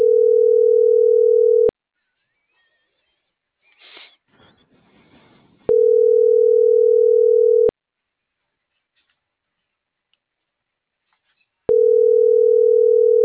Ringback tone (North America), 440 + 480 Hz.